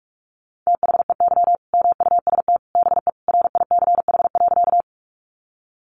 Morse code, heard as a keyed tone, about 44 wpm, 700 Hz. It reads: THEY MUST BE RICH.
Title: Sentences from Top 1100 Words 45wpm